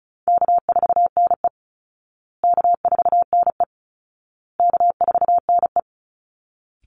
Morse code, heard as a keyed tone, about 34 wpm, 700 Hz. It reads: K4NE K4NE K4NE
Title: Easy Call Signs Repeated 3x - 35wpm